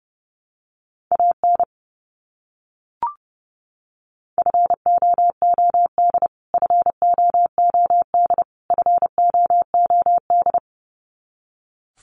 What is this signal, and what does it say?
Morse code, heard as a keyed tone, about 30 wpm, 700 Hz. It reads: AN FOOD FOOD FOOD
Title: Top 300 Words Repeated 30wpm